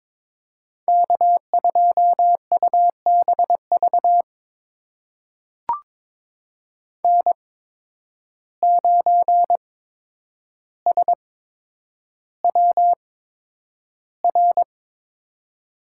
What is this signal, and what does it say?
Morse code, heard as a keyed tone, about 22 wpm, spaced as for 9 wpm, 700 Hz. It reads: K2UBV N9SWR
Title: USA Call Signs - 5 Characters - 8x Character Spacing - 22wpm